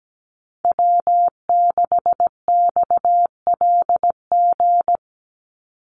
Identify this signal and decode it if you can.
Morse code, heard as a keyed tone, about 17 wpm, 700 Hz. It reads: W6XLG
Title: USA Call Signs 17wpm